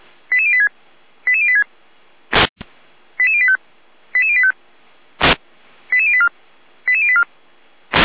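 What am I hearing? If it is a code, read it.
Five-tone selective call (ZVEI), heard as quick runs of five tones, tones of about 70 ms. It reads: 88086 88086 88085 88085 88084 88084
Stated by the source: Five-tone ZVEI selective call received on an SDR.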